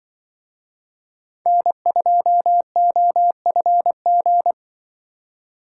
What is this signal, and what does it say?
Morse code, heard as a keyed tone, about 23 wpm, 700 Hz. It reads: N2OFG
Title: Speed Racing - USA Call Signs 20wpm